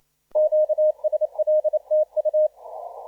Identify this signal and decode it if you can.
Morse code, heard as a keyed tone, about 28 wpm, 620 Hz. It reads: QSL TU
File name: P29NO 1545UTC 20m